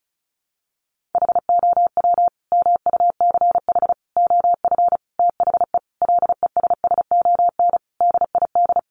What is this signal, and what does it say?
Morse code, heard as a keyed tone, about 34 wpm, 700 Hz. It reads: HOW MUCH OF THE LESSON DID
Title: Sentences from Top 1200 Words 35wpm